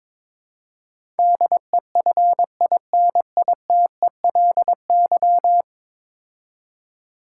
Morse code, heard as a keyed tone, about 22 wpm, 700 Hz. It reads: DEFINITELY
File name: practice / 22wpm / 10 Letter Words - First 3 Letters Given - 22wpm